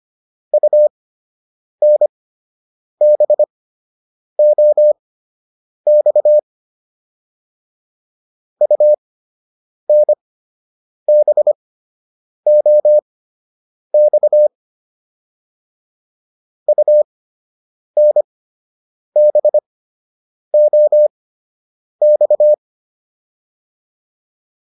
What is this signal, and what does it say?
Morse code, heard as a keyed tone, about 25 wpm, spaced as for 8 wpm, 600 Hz. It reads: UNBOX UNBOX UNBOX